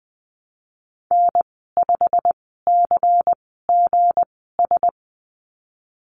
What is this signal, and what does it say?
Morse code, heard as a keyed tone, about 20 wpm, spaced as for 15 wpm, 700 Hz. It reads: N5CGS
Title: 173 Lesson 28g - Number 9 - Call Signs 20wpm